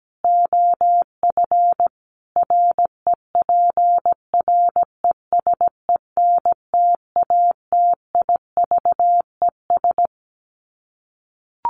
Morse code, heard as a keyed tone, about 17 wpm, 700 Hz. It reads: OF REPRESENTATIVES
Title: Sets of 5 Words Encore 17wpm